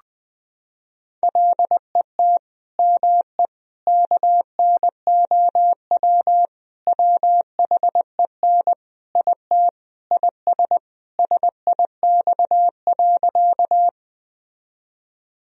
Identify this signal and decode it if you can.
Morse code, heard as a keyed tone, about 20 wpm, 700 Hz. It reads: LET ME KNOW WHEN IT IS SIX.
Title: Sentences from Top 1000 Words - Review Part 2 - 20wpm